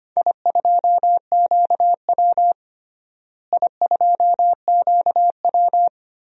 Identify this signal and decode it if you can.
Morse code, heard as a keyed tone, about 25 wpm, 700 Hz. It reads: I2QW I2QW
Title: DXCC-100 Call Signs - 4 Characters - Repeated 3x - 25wpm